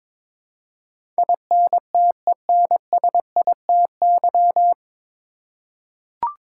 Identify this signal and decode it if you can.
Morse code, heard as a keyed tone, about 22 wpm, 690 Hz. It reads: INTENSITY
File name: 9 Letter Words - First 3 Letters Given - 22wpm